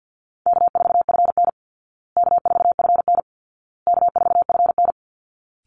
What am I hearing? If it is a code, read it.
Morse code, heard as a keyed tone, about 48 wpm, 700 Hz. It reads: K4FN K4FN K4FN
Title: Easy Call Signs Repeated 3x - 50wpm